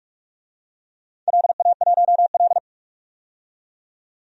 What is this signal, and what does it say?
Morse code, heard as a keyed tone, about 46 wpm, 700 Hz. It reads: LA1L
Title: DXCC-100 Call Signs - 4 Characters - 45wpm